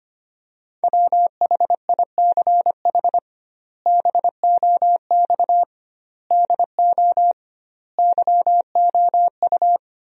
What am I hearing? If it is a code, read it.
Morse code, heard as a keyed tone, about 25 wpm, 700 Hz. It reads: WHICH BOX DO YOU
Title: 2x Word-Spacing - Sentences from Top 500 Words - 25wpm